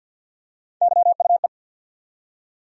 Morse code, heard as a keyed tone, about 53 wpm, 700 Hz. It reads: YUE
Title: USA Call Sign Postfixes - 3 Letters 50wpm